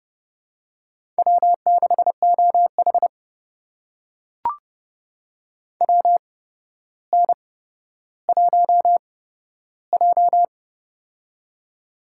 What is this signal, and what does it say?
Morse code, heard as a keyed tone, about 30 wpm, spaced as for 13 wpm, 700 Hz. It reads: W6OH WN1J
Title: USA Call Signs - 4 Characters - 8x Character Spacing - 30wpm